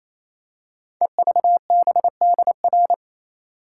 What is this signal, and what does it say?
Morse code, heard as a keyed tone, about 28 wpm, 700 Hz. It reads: EVBDR
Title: Code Groups - 5 Letter:Numbers - 28wpm